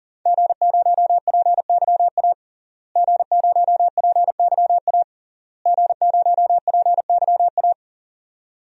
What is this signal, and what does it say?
Morse code, heard as a keyed tone, about 40 wpm, 700 Hz. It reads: G0PYA G0PYA G0PYA
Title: DXCC-100 Call Signs - 5 Characters - Repeated 3x - 40wpm